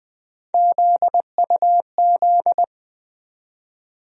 Morse code, heard as a keyed tone, about 20 wpm, 700 Hz. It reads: ZUZ
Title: USA Call Sign Postfixes - 3 Letters 20wpm